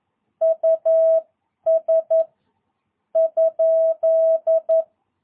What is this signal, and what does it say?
Morse code, heard as a keyed tone, about 11 wpm, spaced as for 8 wpm, 660 Hz. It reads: US?
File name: jh8cbh / 2022cw